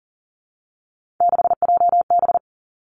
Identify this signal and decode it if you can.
Morse code, heard as a keyed tone, about 40 wpm, 700 Hz. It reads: 6JB
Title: Three Challenging Letter-Number Combinations 40wpm